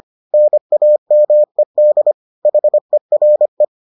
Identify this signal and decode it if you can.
Morse code, heard as a keyed tone, about 25 wpm, 600 Hz. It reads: NAMED HERE